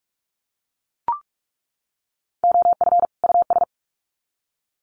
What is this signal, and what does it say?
Morse code, heard as a keyed tone, about 44 wpm, 700 Hz. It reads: OF US
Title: Sets of 2 Words 45wpm